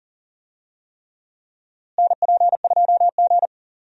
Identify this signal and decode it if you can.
Morse code, heard as a keyed tone, about 40 wpm, 700 Hz. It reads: NP2G
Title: USA Call Signs - 4 Characters - Repeated 3x - 40wpm